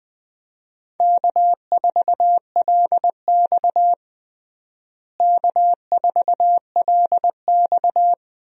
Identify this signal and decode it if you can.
Morse code, heard as a keyed tone, about 20 wpm, 700 Hz. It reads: K4LX K4LX
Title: USA Call Signs - 4 Characters - Repeated 3x - 20wpm